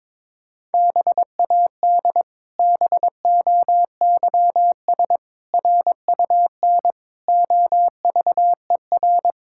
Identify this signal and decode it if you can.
Morse code, heard as a keyed tone, about 22 wpm, 700 Hz. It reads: BAD BOYS RUN OVER
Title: Pangram Challenge v2 - 22wpm